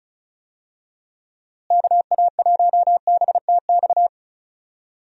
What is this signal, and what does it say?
Morse code, heard as a keyed tone, about 36 wpm, 700 Hz. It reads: KA1BTX
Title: USA Call Signs - 6 Characters - Repeated 2x - 35wpm